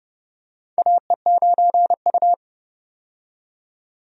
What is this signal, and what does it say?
Morse code, heard as a keyed tone, about 30 wpm, 700 Hz. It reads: AE9U
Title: USA Call Signs - 4 Characters - 30wpm